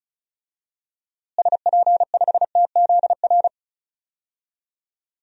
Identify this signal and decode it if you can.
Morse code, heard as a keyed tone, about 36 wpm, 700 Hz. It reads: SP5TZR
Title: DXCC-100 Call Signs - 6 Characters - Repeated 3x - 35wpm